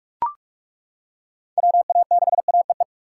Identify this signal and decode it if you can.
Morse code, heard as a keyed tone, about 42 wpm, 700 Hz. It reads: WA6AEE
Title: K1USN SST QSO Practice - 45wpm